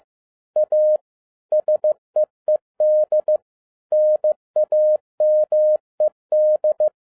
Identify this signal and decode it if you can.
Morse code, heard as a keyed tone, about 15 wpm, 610 Hz. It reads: A SEED NAMED
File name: 2012-01-mp3 / n13-A NEWLY DESIGNED QSL CARD-15wpm